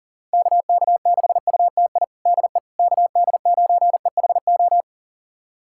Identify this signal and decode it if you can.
Morse code, heard as a keyed tone, about 40 wpm, 700 Hz. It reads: KK6UTI DE KD9EHO
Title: USA Callsign DE Callsign 40wpm